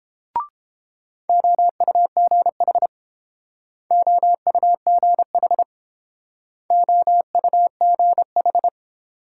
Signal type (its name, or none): Morse code